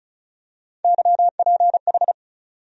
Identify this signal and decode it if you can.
Morse code, heard as a keyed tone, about 34 wpm, 700 Hz. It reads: YPH
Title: USA Call Sign Postfixes - 3 Letters 35wpm